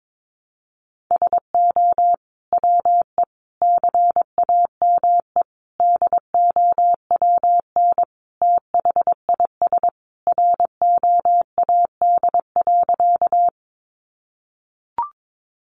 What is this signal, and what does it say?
Morse code, heard as a keyed tone, about 22 wpm, 700 Hz. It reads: SO WE CAME DOWN THIS ROAD.
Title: Sentences from Top 400 Words 22wpm